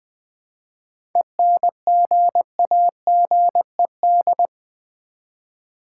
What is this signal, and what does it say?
Morse code, heard as a keyed tone, about 20 wpm, 700 Hz. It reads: ENGAGED
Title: 7 Letter Words - First 3 Letters Given - 20wpm